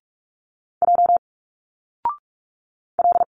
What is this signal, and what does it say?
Morse code, heard as a keyed tone, about 46 wpm, 700 Hz. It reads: J L
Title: Single Letters 45wpm